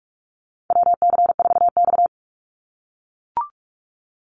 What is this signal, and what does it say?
Morse code, heard as a keyed tone, about 44 wpm, 700 Hz. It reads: WC4X
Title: Call Signs - 45wpm